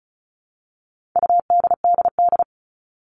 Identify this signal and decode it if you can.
Morse code, heard as a keyed tone, about 34 wpm, 700 Hz. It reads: UDDD
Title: All Cut Numbers - 4-Digit 35wpm